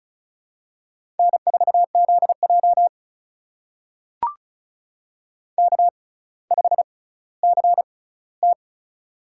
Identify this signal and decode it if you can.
Morse code, heard as a keyed tone, about 36 wpm, spaced as for 18 wpm, 700 Hz. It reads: N4ZJ K5CT
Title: USA Call Signs - 4 Characters - 6x Character Spacing - 35wpm